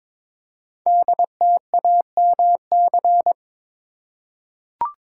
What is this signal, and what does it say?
Morse code, heard as a keyed tone, about 22 wpm, 700 Hz. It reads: DTAMC
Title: Code Groups - 5 Letters - 22wpm